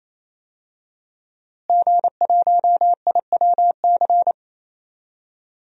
Morse code, heard as a keyed tone, about 28 wpm, 700 Hz. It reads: G1IWC
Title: DXCC-100 Call Signs - 5 Characters - Repeated 3x - 28wpm